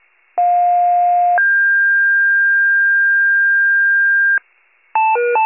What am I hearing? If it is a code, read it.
Two-tone page, heard as a one-second tone, then a three-second tone, about 710 then 1600 Hz. It is heard from 0.4 to 4.4 s.
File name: pager / 46 RESCUE DUTY / wcalert